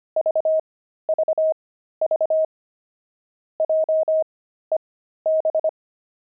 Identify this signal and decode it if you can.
Morse code, heard as a keyed tone, about 25 wpm, spaced as for 13 wpm, 630 Hz. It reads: VVV JEB
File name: pangram-fs-13-cs-25-pavement